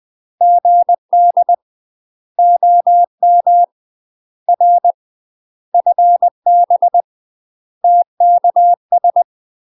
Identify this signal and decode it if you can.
Morse code, heard as a keyed tone, about 20 wpm, 700 Hz. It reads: GD OM R FB TKS
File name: UK MASTER QSOs TRAINING / 20WPM / UK MASTER QSOs 0062